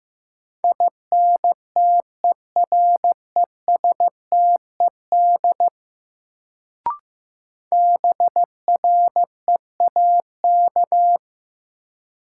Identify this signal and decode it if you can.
Morse code, heard as a keyed tone, about 15 wpm, 700 Hz. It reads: INTERESTED BREAK
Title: Sentences from Top 900 Words - New Word Drill - 15wpm